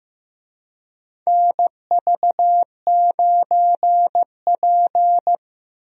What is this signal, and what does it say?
Morse code, heard as a keyed tone, about 15 wpm, 700 Hz. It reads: NV9P
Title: USA Call Signs - 4 Characters - 15wpm